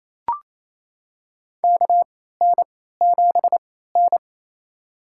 Morse code, heard as a keyed tone, about 28 wpm, spaced as for 16 wpm, 700 Hz. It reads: KN7N
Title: USA Call Signs - 4 Characters - 3x Character Spacing - 28wpm